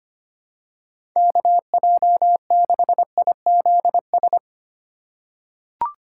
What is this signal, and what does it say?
Morse code, heard as a keyed tone, about 25 wpm, 700 Hz. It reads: KJ6IZS
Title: USA Call Signs - 6 Characters - 2x Character Spacing - 25wpm